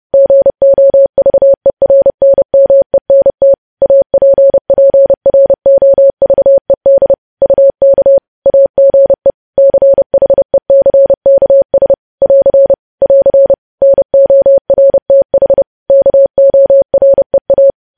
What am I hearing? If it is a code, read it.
Morse code, heard as a keyed tone, about 30 wpm, 550 Hz. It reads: GOVERNMENT APPROVED UK AGE CHECKS <AR> <AR> NORTH KOREA
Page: News Headlines in Morse Code at 30 WPM